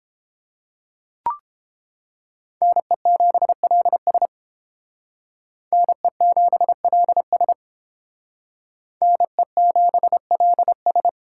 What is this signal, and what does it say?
Morse code, heard as a keyed tone, about 30 wpm, 700 Hz. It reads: NE7LS NE7LS NE7LS